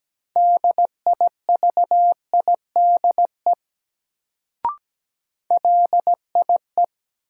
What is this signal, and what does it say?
Morse code, heard as a keyed tone, about 17 wpm, 700 Hz. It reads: DIVIDE LIE
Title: Sentences from Top 1600 Words - New Word Drill Alt - 17wpm